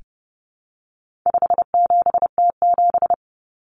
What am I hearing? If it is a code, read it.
Morse code, heard as a keyed tone, about 30 wpm, 700 Hz. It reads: 57T7
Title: Common Cut Numbers - 4-Digit 30wpm